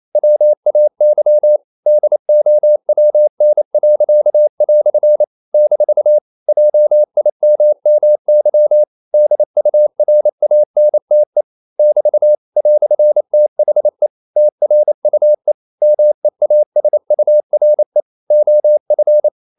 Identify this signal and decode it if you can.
Morse code, heard as a keyed tone, about 28 wpm, 600 Hz. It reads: WAY DOWN." - JIMMY DURANTE <BT> "THE TRUE MEASURE OF
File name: CW / Practice / EnglishQuotations0228